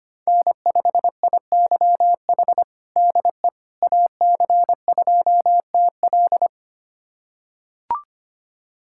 Morse code, heard as a keyed tone, about 25 wpm, 700 Hz. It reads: N5IYH DE AC2TL
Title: USA Callsign DE Callsign - 5 Characters - 25wpm